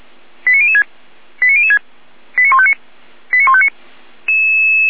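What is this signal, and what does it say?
Five-tone selective call (ZVEI), heard as quick runs of five tones, tones of about 70 ms. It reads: 89006 89006 89159 89159
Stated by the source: Received on an SDR as a ZVEI five-tone call.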